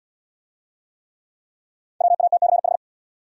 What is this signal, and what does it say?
Morse code, heard as a keyed tone, about 77 wpm, 700 Hz. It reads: WATCH